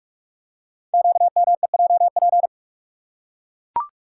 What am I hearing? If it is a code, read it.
Morse code, heard as a keyed tone, about 44 wpm, 700 Hz. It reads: QMEJP
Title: Code Groups - 5 Letter:Numbers - 45wpm